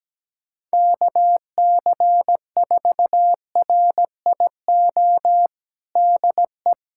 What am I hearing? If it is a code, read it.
Morse code, heard as a keyed tone, about 17 wpm, 700 Hz. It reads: KC4RIO DE
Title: USA Callsign DE Callsign 17wpm